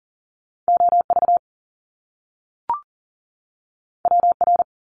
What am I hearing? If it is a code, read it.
Morse code, heard as a keyed tone, about 38 wpm, 700 Hz. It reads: OV WR
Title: Top Two Letter Combinations in Words 40wpm